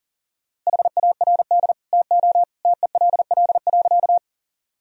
Morse code, heard as a keyed tone, about 40 wpm, 700 Hz. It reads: HARD TO TELL.
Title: Sentences from Top 1000 Words - Review Part 1 - 40wpm